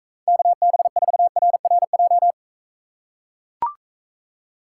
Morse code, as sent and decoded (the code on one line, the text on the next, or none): -.- -... ....- .-. .-. .---
KB4RRJ